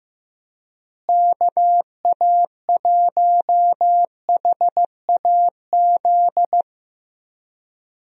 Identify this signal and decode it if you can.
Morse code, heard as a keyed tone, about 14 wpm, 700 Hz. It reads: KA1HAZ
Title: USA Call Signs - 6 Characters - Repeated 2x - 15wpm